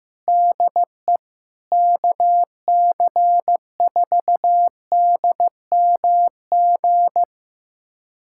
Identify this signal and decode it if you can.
Morse code, heard as a keyed tone, about 15 wpm, 700 Hz. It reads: DE KC4DMG
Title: USA Callsign DE Callsign - 6 Characters - 15wpm